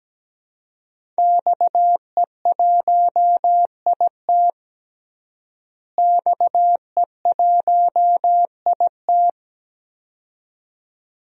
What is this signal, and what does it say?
Morse code, heard as a keyed tone, about 17 wpm, 700 Hz. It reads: XE1IT XE1IT
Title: DXCC-100 Call Signs - 5 Characters - Repeated 2x - 17wpm